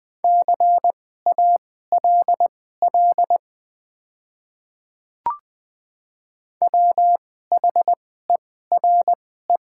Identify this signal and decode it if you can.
Morse code, heard as a keyed tone, about 20 wpm, spaced as for 14 wpm, 700 Hz. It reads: CALL WHERE
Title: 2x Character Spacing - Top 200 Words - 20wpm